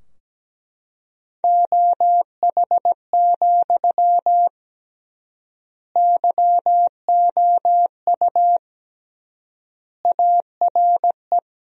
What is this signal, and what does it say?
Morse code, heard as a keyed tone, about 17 wpm, 700 Hz. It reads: OH, YOU ARE
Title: 3x Word-Spacing - Sentences from Top 500 Words - 17wpm